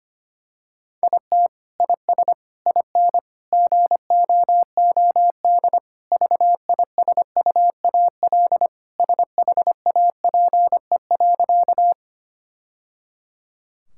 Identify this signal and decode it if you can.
Morse code, heard as a keyed tone, about 25 wpm, 700 Hz. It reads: IT IS IN GOOD VISUAL SHAPE.
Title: Sentences from Top 1000 Words - Review Part 6 - 25wpm